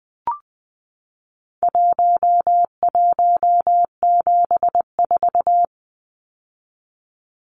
Morse code, heard as a keyed tone, about 20 wpm, 700 Hz. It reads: 1174
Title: Numbers - Contest Style - 4-Digit 20wpm